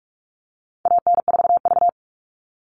Morse code, heard as a keyed tone, about 44 wpm, 700 Hz. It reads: AN4V
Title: All Cut Numbers - 4-Digit 45wpm